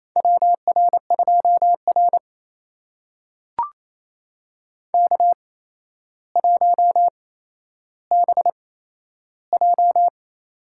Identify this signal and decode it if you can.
Morse code, heard as a keyed tone, about 28 wpm, spaced as for 12 wpm, 700 Hz. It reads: WR2R K1BJ
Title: USA Call Signs - 4 Characters - 8x Character Spacing - 28wpm